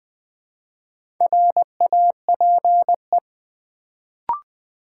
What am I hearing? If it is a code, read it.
Morse code, heard as a keyed tone, about 20 wpm, 700 Hz. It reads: RAPE
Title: Build A Word - 1 Letter At A Time - 3 to 5-Letter Words - 20wpm